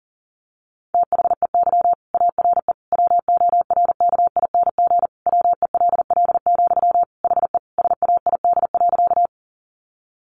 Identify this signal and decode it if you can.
Morse code, heard as a keyed tone, about 40 wpm, 700 Hz. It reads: THEY ARE WORKING WELL, HE SAID.
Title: Sentences from Top 700 Words 40wpm